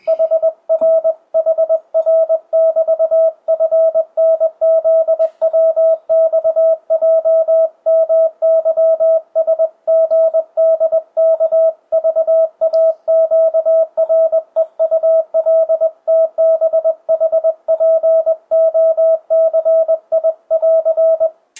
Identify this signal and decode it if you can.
Morse code, heard as a keyed tone, about 20 wpm, 650 Hz. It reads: HRHR<BT>FNZWXJMYSGDKVAQREULTBHPOCI<AR>
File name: 2025cw / speedup / 21wpm